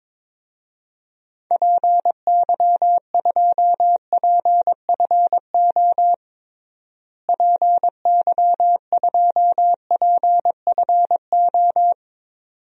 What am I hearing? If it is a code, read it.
Morse code, heard as a keyed tone, about 22 wpm, 700 Hz. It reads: PY2PFO PY2PFO
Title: DXCC-100 Call Signs - 6 Characters - Repeated 2x - 22wpm